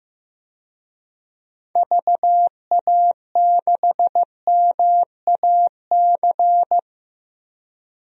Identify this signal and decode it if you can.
Morse code, heard as a keyed tone, about 15 wpm, 700 Hz. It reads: VA6MAC
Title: DXCC-100 Call Signs - 6 Characters - Repeated 3x - 15wpm